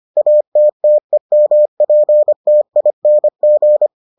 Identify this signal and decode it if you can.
Morse code, heard as a keyed tone, about 25 wpm, 600 Hz. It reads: ATTEMPTING